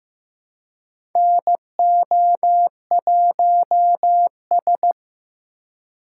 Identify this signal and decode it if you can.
Morse code, heard as a keyed tone, about 15 wpm, 700 Hz. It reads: NO1S
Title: USA Call Signs - 3x Character Spacing - 15wpm